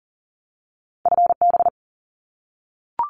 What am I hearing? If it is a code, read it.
Morse code, heard as a keyed tone, about 39 wpm, 700 Hz. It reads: FB
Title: Abbreviations 40wpm